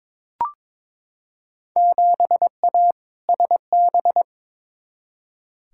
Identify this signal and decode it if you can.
Morse code, heard as a keyed tone, about 22 wpm, 700 Hz. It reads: 7A SB
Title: Field Day Exchanges Encore - 22wpm